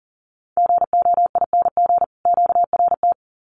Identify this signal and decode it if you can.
Morse code, heard as a keyed tone, about 40 wpm, 700 Hz. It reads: GOING QRT
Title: Common QSO Phrases 40wpm